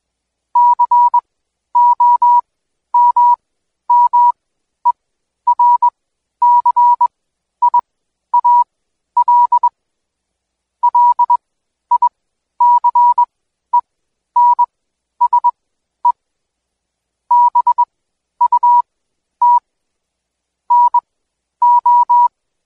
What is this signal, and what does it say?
Morse code, heard as a keyed tone, about 21 wpm, spaced as for 11 wpm, 970 Hz. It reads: COMMERCIAL LICENSE BUT NO
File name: Code Quick / AudioDownld2 / Seg120-Day 25-DayDrill 25